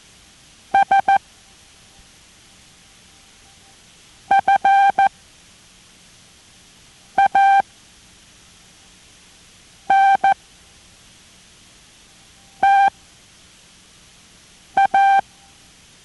Morse code, heard as a keyed tone, about 14 wpm, 790 Hz. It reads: S F A N T A